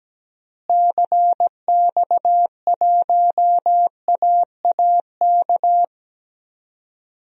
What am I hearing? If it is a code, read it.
Morse code, heard as a keyed tone, about 17 wpm, 700 Hz. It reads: CX1AAK
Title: DXCC-100 Call Signs - 6 Characters - Repeated 2x - 17wpm